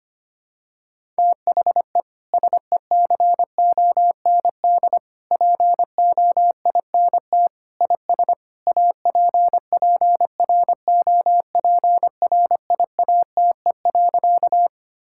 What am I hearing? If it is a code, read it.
Morse code, heard as a keyed tone, about 25 wpm, 700 Hz. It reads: THE SECOND POINT IS APPROPRIATE.
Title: Sentences from Top 1000 Words - Review Part 5 - 25wpm